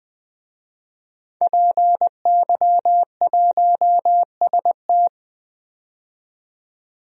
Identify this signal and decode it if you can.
Morse code, heard as a keyed tone, about 20 wpm, 700 Hz. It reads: PY1ST
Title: DXCC-100 Call Signs - 5 Characters - 20wpm